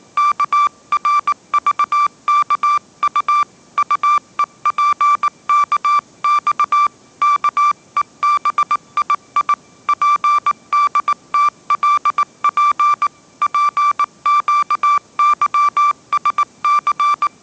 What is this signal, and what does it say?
Morse code, heard as a keyed tone, about 19 wpm, 1170 Hz. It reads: KRVKU UEPKX KEBII PDTLP PQYSC
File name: mrx083 20wpm